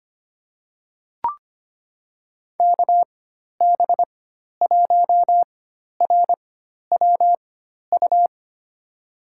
Morse code, heard as a keyed tone, about 25 wpm, spaced as for 12 wpm, 700 Hz. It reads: KB1RWU
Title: USA Call Signs - 4x Character Spacing - 25wpm